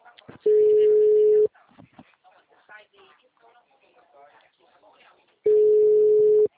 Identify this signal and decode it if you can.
Ringback tone (Europe, 425 Hz), with speech.